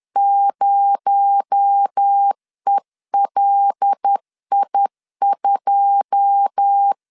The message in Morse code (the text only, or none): 0ELI2